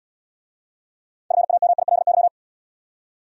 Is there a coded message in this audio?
Yes: Morse code from 1.3 to 2.3 s, heard as a keyed tone, about 68 wpm, 700 Hz. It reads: FAMILY